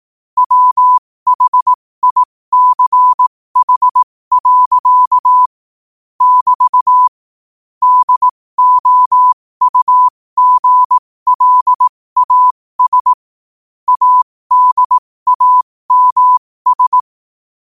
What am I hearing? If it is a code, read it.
Morse code, heard as a keyed tone, about 18 wpm, spaced as for 15 wpm, 1000 Hz. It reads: WHICH. <BT> DOUGLAS ADAMS
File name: qotdfiles / 15wpm